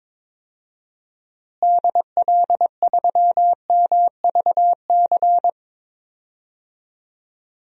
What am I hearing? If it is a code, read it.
Morse code, heard as a keyed tone, about 22 wpm, 700 Hz. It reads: DL3MVC
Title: DXCC-100 Call Signs - 6 Characters - Repeated 3x - 22wpm